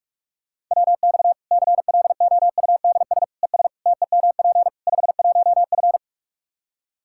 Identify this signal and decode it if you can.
Morse code, heard as a keyed tone, about 41 wpm, 700 Hz. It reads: WX CLOUDS ES TEMP 51F
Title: Practice QSO - Exchange 2 - Info - 45wpm